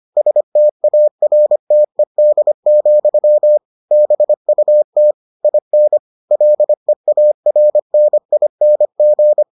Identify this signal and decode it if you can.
Morse code, heard as a keyed tone, about 25 wpm, 600 Hz. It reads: STARTED, BUT IN LEARNING